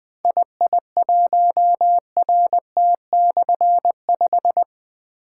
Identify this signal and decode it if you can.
Morse code, heard as a keyed tone, about 20 wpm, 700 Hz. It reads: II1RT/5
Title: Difficult Call Signs 20wpm